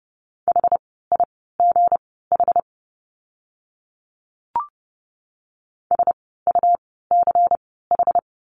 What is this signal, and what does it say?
Morse code, heard as a keyed tone, about 30 wpm, spaced as for 17 wpm, 700 Hz. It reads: HIGH SUCH
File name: course / 30:10wpm / 237 Lesson 38f - Letter q - Words 30wpm